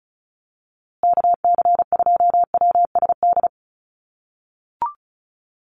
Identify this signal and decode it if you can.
Morse code, heard as a keyed tone, about 34 wpm, 700 Hz. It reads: KC2WSD
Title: USA Call Signs Repeated 3x 35wpm